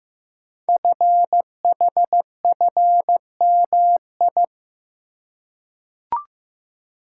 Morse code, heard as a keyed tone, about 15 wpm, 700 Hz. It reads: FHFMI
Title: Code Groups - 5 Letter:Numbers - 15wpm